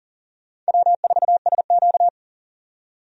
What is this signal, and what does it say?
Morse code, heard as a keyed tone, about 40 wpm, 700 Hz. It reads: W4SQ
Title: USA Call Signs - 4 Characters - 40wpm